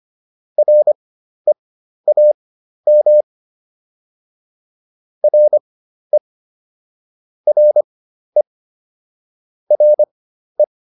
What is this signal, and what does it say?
Morse code, heard as a keyed tone, about 25 wpm, spaced as for 12 wpm, 600 Hz. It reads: REAM RE RE RE